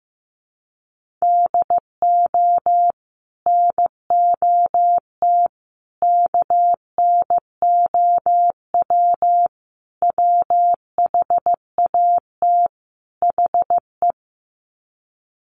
Morse code, heard as a keyed tone, about 15 wpm, 700 Hz. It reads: DO NOT KNOW WHAT HE
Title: Sets of 5 Words Encore2 15wpm